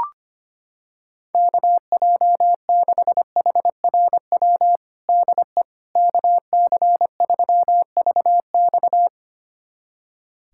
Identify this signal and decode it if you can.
Morse code, heard as a keyed tone, about 25 wpm, 700 Hz. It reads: KJ6HRW DE KC3VX
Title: USA Callsign DE Callsign 25wpm